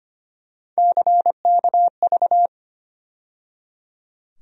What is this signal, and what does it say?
Morse code, heard as a keyed tone, about 25 wpm, 700 Hz. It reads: CKV
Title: USA Call Sign Postfixes - 3 Letters 25wpm